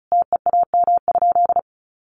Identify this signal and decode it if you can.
Morse code, heard as a keyed tone, about 34 wpm, 700 Hz. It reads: TEAM?
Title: Sentences from Top 1500 Words 35wpm